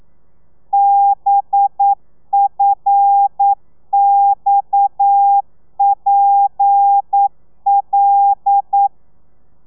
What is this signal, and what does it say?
Morse code, heard as a keyed tone, about 9 wpm, 800 Hz. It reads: BFXPL
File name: telegrafia / i2tfj / Lez7 40